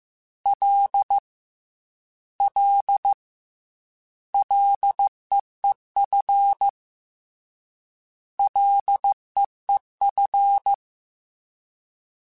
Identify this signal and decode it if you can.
Morse code, heard as a keyed tone, about 15 wpm, 800 Hz. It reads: L L LEEF LEEF
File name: Ls / Single L m3s / Single L @13